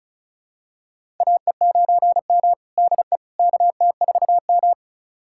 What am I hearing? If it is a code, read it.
Morse code, heard as a keyed tone, about 34 wpm, 690 Hz. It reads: AE9M DE KT4M
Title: USA Callsign DE Callsign - 4 Characters - 35wpm